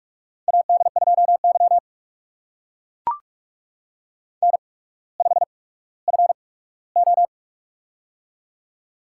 Morse code, heard as a keyed tone, about 44 wpm, spaced as for 19 wpm, 700 Hz. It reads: AD2Y N5FO
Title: USA Call Signs - 4 Characters - 8x Character Spacing - 45wpm